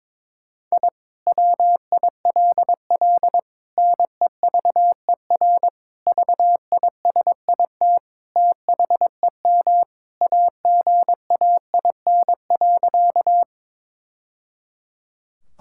Morse code, heard as a keyed tone, about 22 wpm, 700 Hz. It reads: I WILL NEVER VISIT THEM AGAIN.
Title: Sentences from Top 1000 Words - Review Part 6 - 22wpm